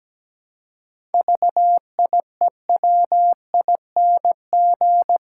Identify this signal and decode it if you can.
Morse code, heard as a keyed tone, about 17 wpm, 700 Hz. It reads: VIEWING
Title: Pangram Challenge v2 - 17wpm